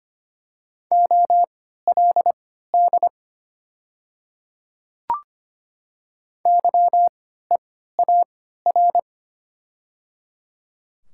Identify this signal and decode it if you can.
Morse code, heard as a keyed tone, about 25 wpm, spaced as for 14 wpm, 690 Hz. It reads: OLD YEAR
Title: 3x Character Spacing - Top 200 Words - 25wpm